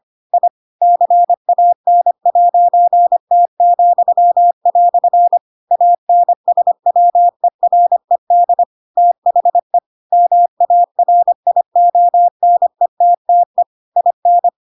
Morse code, heard as a keyed tone, about 25 wpm, 700 Hz. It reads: I CAN'T," ANSWERED THE MARIONETTE IN
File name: Pinocchio / Pinocchio ch7 25 25 ik1hge